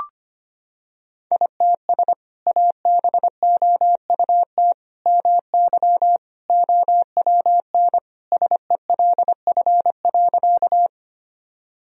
Morse code, heard as a keyed tone, about 25 wpm, 700 Hz. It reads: ITS ABOUT MY OWN SELF.
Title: Sentences from Top 1000 Words - Review Part 1 - 25wpm